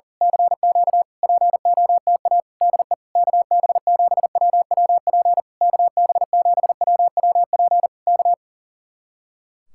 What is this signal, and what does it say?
Morse code, heard as a keyed tone, about 38 wpm, 700 Hz. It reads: CQ POTA DE KB7WWP KB7WWP K
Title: Speed-Racing Common QSO Phrases Encore - 40wpm